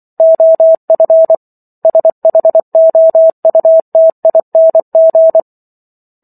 Morse code, heard as a keyed tone, about 24 wpm, 650 Hz. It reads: OF SHOUTING